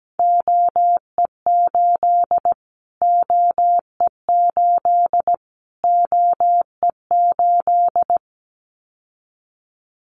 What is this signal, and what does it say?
Morse code, heard as a keyed tone, about 17 wpm, 700 Hz. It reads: OE8 OE8 OE8
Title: Callsign Prefixes for Top 25 DXCC - Repeated 3x - 17wpm